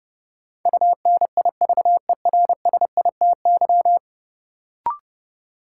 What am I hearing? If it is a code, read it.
Morse code, heard as a keyed tone, about 32 wpm, 700 Hz. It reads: UNIVERSITY
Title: Sentences from Top 1000 Words - Review Part 4 - New Word Drill - 30wpm